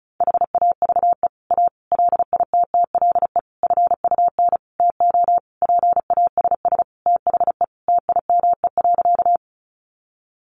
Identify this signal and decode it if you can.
Morse code, heard as a keyed tone, about 34 wpm, 700 Hz. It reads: HAVE A LITTLE FUN TO PASS THE TIME.
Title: Sentences from Top 700 Words 35wpm